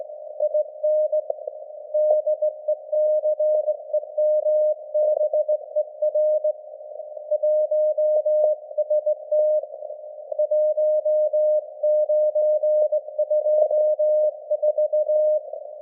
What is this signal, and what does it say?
Morse code, heard as a keyed tone, about 19 wpm, 610 Hz. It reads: IN DECEMBER 1ST 1924
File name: laengstwellenempfang / SDRSharp 17kHz AF